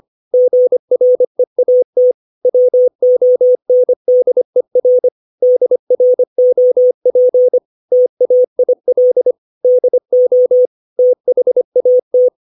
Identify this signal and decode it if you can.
Morse code, heard as a keyed tone, about 25 wpm, 500 Hz. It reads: GREAT WONDER DROP TAIL DO THAT
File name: 632 words / 632 Words 25WPM